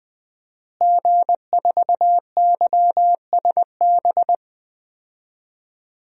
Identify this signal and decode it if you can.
Morse code, heard as a keyed tone, about 20 wpm, 700 Hz. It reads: G4YSB
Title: DXCC-100 Call Signs - 5 Characters - Repeated 2x - 20wpm